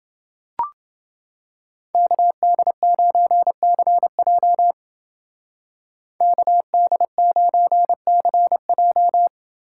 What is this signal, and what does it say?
Morse code, heard as a keyed tone, about 28 wpm, 700 Hz. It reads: KD9CJ KD9CJ